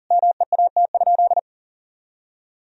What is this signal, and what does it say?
Morse code, heard as a keyed tone, about 40 wpm, 700 Hz. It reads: MEAT?
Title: Sentences from Top 1000 Words - Review Part 3 - 40wpm